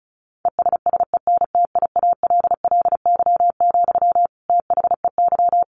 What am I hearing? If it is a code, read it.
Morse code, heard as a keyed tone, about 34 wpm, 700 Hz. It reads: ESSENTIALLY, THEY
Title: Sentences from Top 1300 Words 35wpm